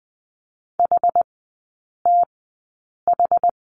Morse code, heard as a keyed tone, about 20 wpm, spaced as for 8 wpm, 700 Hz. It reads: HTH
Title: 060 Lesson 11e - Letter h - Three Characters 20wpm